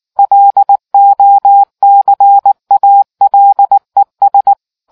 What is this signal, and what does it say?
Morse code, heard as a keyed tone, about 19 wpm, 800 Hz. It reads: LOCALES